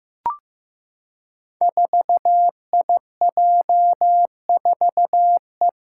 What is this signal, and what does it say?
Morse code, heard as a keyed tone, about 15 wpm, 700 Hz. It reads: 4IJ4E
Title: Five Challenging Letter-Number Combinations 15wpm